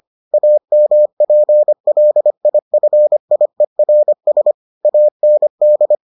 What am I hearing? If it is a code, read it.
Morse code, heard as a keyed tone, about 25 wpm, 600 Hz. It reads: AMPLIFIERS AND